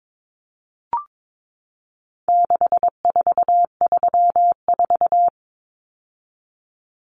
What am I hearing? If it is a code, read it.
Morse code, heard as a keyed tone, about 22 wpm, 700 Hz. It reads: 6434
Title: Numbers - 4-Digit 22wpm